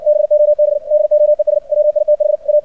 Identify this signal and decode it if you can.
Morse code, heard as a keyed tone, about 59 wpm, 600 Hz. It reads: BOB DOES WITH A
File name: archived / CRSnet